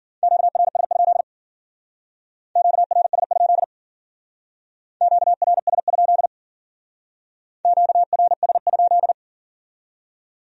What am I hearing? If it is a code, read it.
Morse code, heard as a keyed tone, about 50 wpm, 700 Hz. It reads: QRS? QRS? QRS? QRS?